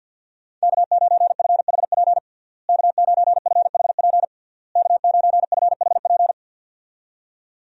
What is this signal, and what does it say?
Morse code, heard as a keyed tone, about 51 wpm, 700 Hz. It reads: K9FHP K9FHP K9FHP
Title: USA Call Signs - 5 Characters - Repeated 3x - 50wpm